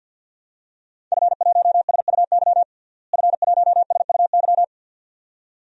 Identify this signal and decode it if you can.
Morse code, heard as a keyed tone, about 51 wpm, 700 Hz. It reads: F1SUY F1SUY
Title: DXCC-100 Call Signs - 5 Characters - Repeated 2x - 50wpm